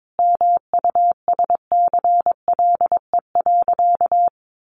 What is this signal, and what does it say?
Morse code, heard as a keyed tone, about 22 wpm, 700 Hz. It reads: MUSCLE.
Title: Sentences from Top 1300 Words 22wpm